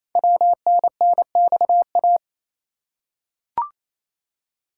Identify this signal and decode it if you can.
Morse code, heard as a keyed tone, about 29 wpm, 700 Hz. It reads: WNNXA
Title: Code Groups - 5 Letters - 28wpm